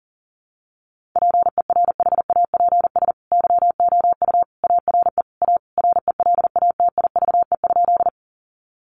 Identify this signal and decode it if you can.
Morse code, heard as a keyed tone, about 40 wpm, 700 Hz. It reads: PERHAPS YOU ARE A RELATIVE?
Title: Sentences from Top 1400 Words 40wpm